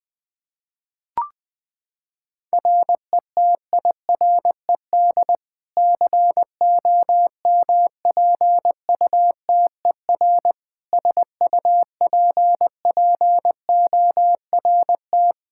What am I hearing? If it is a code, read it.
Morse code, heard as a keyed tone, about 20 wpm, 700 Hz. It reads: RETIRED COMPUTER SUPPORT
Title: Practice QSO - Exchange 3 - More Info - 20wpm